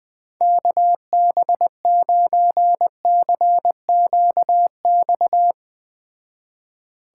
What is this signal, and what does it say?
Morse code, heard as a keyed tone, about 20 wpm, 700 Hz. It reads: KB9CQX
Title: Speed Racing - USA 2x3 Call Signs 17wpm